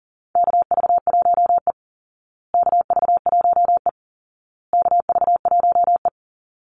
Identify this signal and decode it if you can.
Morse code, heard as a keyed tone, about 40 wpm, 700 Hz. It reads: KV1E KV1E KV1E
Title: Call Signs Repeated 2x - 40wpm